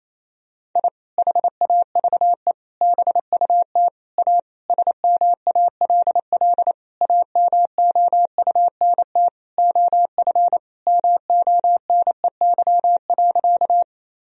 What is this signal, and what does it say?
Morse code, heard as a keyed tone, about 28 wpm, 700 Hz. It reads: I HAVE BUT A SMALL AMOUNT OF MONEY.
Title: Sentences from Top 1000 Words - Review Part 4 - 28wpm